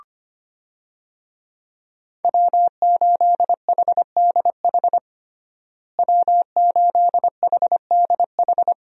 Morse code, heard as a keyed tone, about 25 wpm, 700 Hz. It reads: W8HDH W8HDH
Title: USA Call Signs - 5 Characters - Repeated 2x - 25wpm